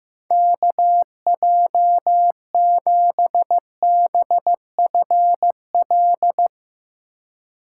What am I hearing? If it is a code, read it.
Morse code, heard as a keyed tone, about 15 wpm, 700 Hz. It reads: KJ7BFL
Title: USA Call Signs - 6 Characters - 2x Character Spacing - 15wpm